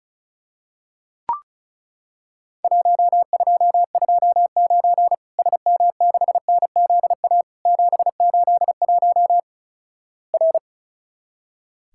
Morse code, heard as a keyed tone, about 36 wpm, 700 Hz. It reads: 1229 SM6NZA 781 R
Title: Worked All Europe DX Contest - QTC Exchange - 35wpm